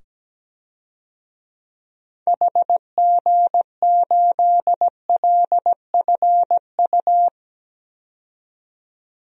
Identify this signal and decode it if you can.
Morse code, heard as a keyed tone, about 17 wpm, 700 Hz. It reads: HG8LFU
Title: DXCC-100 Call Signs - 6 Characters - 17wpm